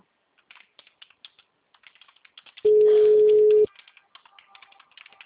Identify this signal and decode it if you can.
Ringback tone (Europe, 425 Hz).
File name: out- -2006- - - .48642